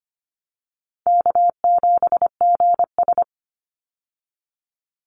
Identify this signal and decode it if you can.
Morse code, heard as a keyed tone, about 25 wpm, 700 Hz. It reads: K7GS
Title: Easy Call Signs - 25wpm